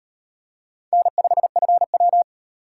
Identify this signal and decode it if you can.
Morse code, heard as a keyed tone, about 38 wpm, 700 Hz. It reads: N5FW